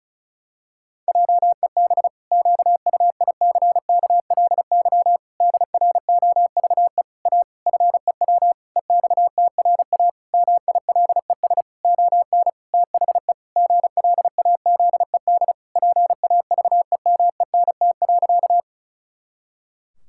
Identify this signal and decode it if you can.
Morse code, heard as a keyed tone, about 34 wpm, 700 Hz. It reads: JEB QUICKLY DROVE A FEW EXTRA MILES ON THE GLAZED PAVEMENT.
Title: Pangram Challenge v2 - 35wpm